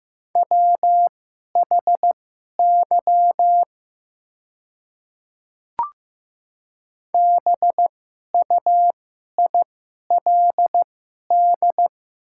Morse code, heard as a keyed tone, about 15 wpm, spaced as for 11 wpm, 700 Hz. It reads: WHY BUILD
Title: 2x Character Spacing - Top 200 Words - 15wpm